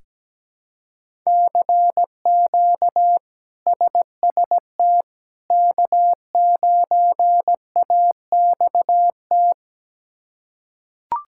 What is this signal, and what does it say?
Morse code, heard as a keyed tone, about 17 wpm, 700 Hz. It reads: CQ SST K9AXT
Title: K1USN SST QSO Practice - 17wpm